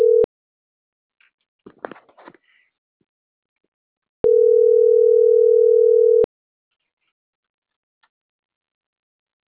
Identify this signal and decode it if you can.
Ringback tone (North America), 440 + 480 Hz.